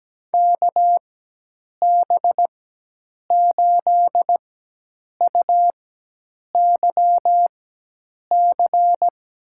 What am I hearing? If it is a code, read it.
Morse code, heard as a keyed tone, about 17 wpm, spaced as for 8 wpm, 700 Hz. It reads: KB8UYC
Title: USA Call Signs - 6 Characters - 4x Character Spacing - 17wpm